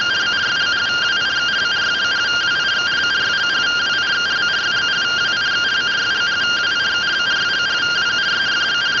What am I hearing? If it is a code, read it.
Two-tone teleprinter signal, about 170 Hz shift.